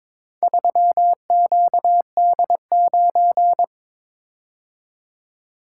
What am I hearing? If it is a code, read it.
Morse code, heard as a keyed tone, about 22 wpm, 700 Hz. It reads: 3QD9
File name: practice / 22wpm / Four Letter-Number Combinations - 22wpm